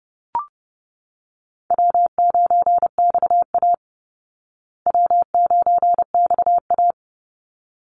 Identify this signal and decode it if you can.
Morse code, heard as a keyed tone, about 28 wpm, 700 Hz. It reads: W9XA W9XA
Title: Easy Call Signs Repeated 2x - 30wpm